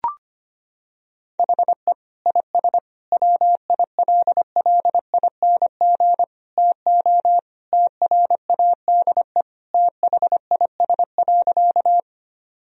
Morse code, heard as a keyed tone, about 25 wpm, 700 Hz. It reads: HE IS WILLING TO TRADE THIS.
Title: Sentences from Top 1000 Words - Review Part 4 - 25wpm